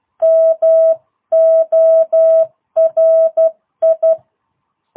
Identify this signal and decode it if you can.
Morse code, heard as a keyed tone, about 12 wpm, 650 Hz. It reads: MORI
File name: jh8cbh / 2022cw